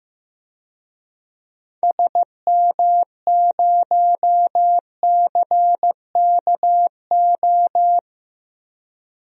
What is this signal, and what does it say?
Morse code, heard as a keyed tone, about 16 wpm, 700 Hz. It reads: SM0CKO
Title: DXCC-100 Call Signs - 6 Characters - 15wpm